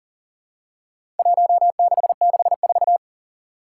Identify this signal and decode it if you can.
Morse code, heard as a keyed tone, about 42 wpm, 700 Hz. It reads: 1664
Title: Four Challenging Letter-Number Combinations 40wpm